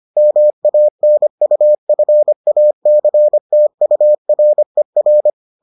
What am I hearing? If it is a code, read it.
Morse code, heard as a keyed tone, about 25 wpm, 600 Hz. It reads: MANUFACTURER